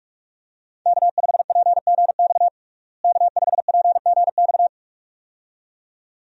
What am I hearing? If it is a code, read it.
Morse code, heard as a keyed tone, about 46 wpm, 700 Hz. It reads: K5PGX K5PGX
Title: USA Call Signs - 5 Characters - Repeated 2x - 45wpm